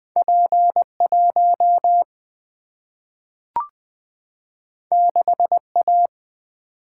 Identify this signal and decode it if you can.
Morse code, heard as a keyed tone, about 20 wpm, 700 Hz. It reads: P1 6A
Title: Single-Letter Single-Number 20wpm